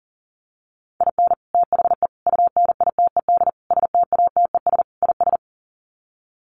Morse code, heard as a keyed tone, about 37 wpm, 700 Hz. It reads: IN THE UNITED STATES IS
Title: Sets of 5 Words 40wpm